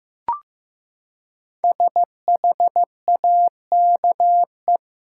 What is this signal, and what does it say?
Morse code, heard as a keyed tone, about 15 wpm, 700 Hz. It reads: SHAKE
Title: Five Letter Words - 15wpm